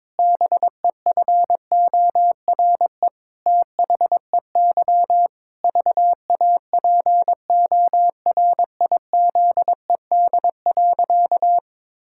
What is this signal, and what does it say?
Morse code, heard as a keyed tone, about 22 wpm, 700 Hz. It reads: BEFORE THEY VAPORIZED.
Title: Pangram Challenge v2 - 22wpm